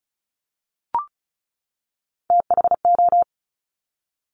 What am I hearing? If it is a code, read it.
Morse code, heard as a keyed tone, about 36 wpm, 700 Hz. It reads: THO
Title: Top Three Letter Combinations in Words 35wpm